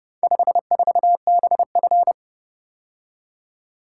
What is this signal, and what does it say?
Morse code, heard as a keyed tone, about 29 wpm, 700 Hz. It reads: 54BF